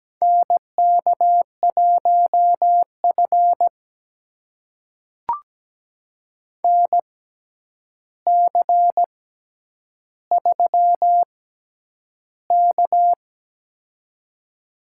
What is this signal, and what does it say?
Morse code, heard as a keyed tone, about 17 wpm, spaced as for 9 wpm, 700 Hz. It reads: NK1F NC3K
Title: USA Call Signs - 4 Characters - 6x Character Spacing - 17wpm